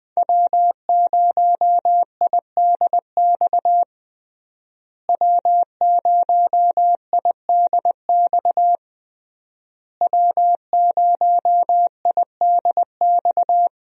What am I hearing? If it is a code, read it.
Morse code, heard as a keyed tone, about 20 wpm, 700 Hz. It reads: W0IDX W0IDX W0IDX
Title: USA Call Signs - 5 Characters - Repeated 3x - 20wpm